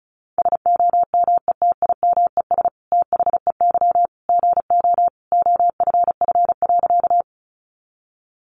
Morse code, heard as a keyed tone, about 34 wpm, 700 Hz. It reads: SOMETIMES THEY GO OFF.
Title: Sentences from Top 600 Words 35wpm